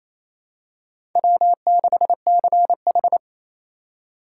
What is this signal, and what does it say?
Morse code, heard as a keyed tone, about 28 wpm, 700 Hz. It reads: W6CH
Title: USA Call Signs - 3x Character Spacing - 28wpm